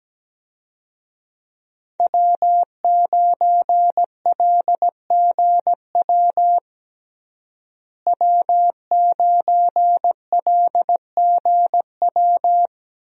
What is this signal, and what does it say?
Morse code, heard as a keyed tone, about 17 wpm, 700 Hz. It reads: W9LGW W9LGW
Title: USA Call Signs - 5 Characters - Repeated 2x - 17wpm